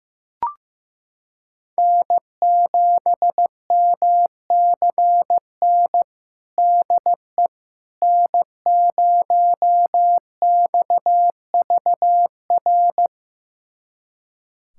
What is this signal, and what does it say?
Morse code, heard as a keyed tone, about 15 wpm, 700 Hz. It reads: N7MCN DE N0XVR
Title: USA Callsign DE Callsign - 5 Characters - 15wpm